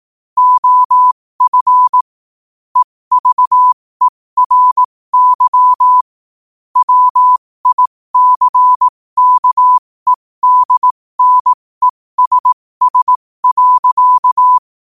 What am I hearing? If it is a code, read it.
Morse code, heard as a keyed tone, about 18 wpm, spaced as for 15 wpm, 1000 Hz. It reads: OF EVERY WICKEDNESS.